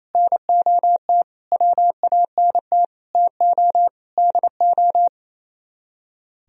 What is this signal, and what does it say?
Morse code, heard as a keyed tone, about 36 wpm, 700 Hz. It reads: NOT WANT TO DO